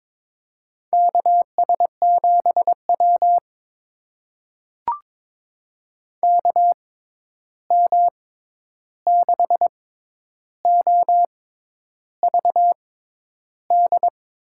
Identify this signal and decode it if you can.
Morse code, heard as a keyed tone, about 22 wpm, spaced as for 10 wpm, 700 Hz. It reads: KS7W KM6OVD
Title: USA Call Signs - 6x Character Spacing - 22wpm